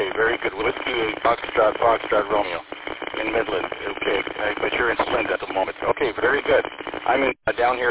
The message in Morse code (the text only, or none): none